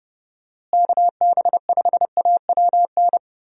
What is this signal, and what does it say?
Morse code, heard as a keyed tone, about 30 wpm, 700 Hz. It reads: KB5AWN
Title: USA Call Signs - 6 Characters - Repeated 3x - 30wpm